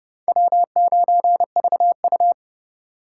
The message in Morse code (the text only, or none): W9VU